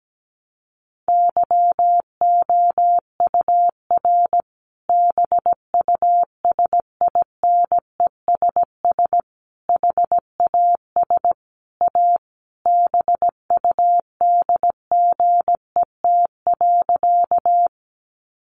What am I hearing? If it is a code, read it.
Morse code, heard as a keyed tone, about 17 wpm, 700 Hz. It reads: YOUR BUSINESS HAS A BUDGET.
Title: Sentences from Top 1200 Words 17wpm